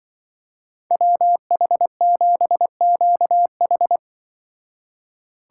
Morse code, heard as a keyed tone, about 24 wpm, 700 Hz. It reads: WH7QH
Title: Speed Racing - USA 2x2 Call Signs 20wpm